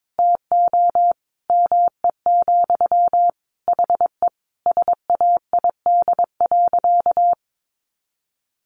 Morse code, heard as a keyed tone, about 22 wpm, 700 Hz. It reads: TO ME, HE SAID.
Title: Sentences from Top 600 Words 22wpm